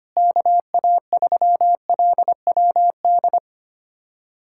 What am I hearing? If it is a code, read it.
Morse code, heard as a keyed tone, about 25 wpm, 700 Hz. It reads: KA3LWD
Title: USA Callsign DE Callsign 25wpm